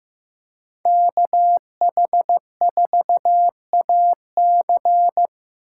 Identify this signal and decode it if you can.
Morse code, heard as a keyed tone, about 15 wpm, 700 Hz. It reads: KH4AC
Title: USA Call Signs - 4x Character Spacing - 15wpm